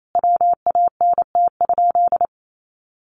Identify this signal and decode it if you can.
Morse code, heard as a keyed tone, about 28 wpm, 700 Hz. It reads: WANT?
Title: Sentences from Top 1100 Words 28wpm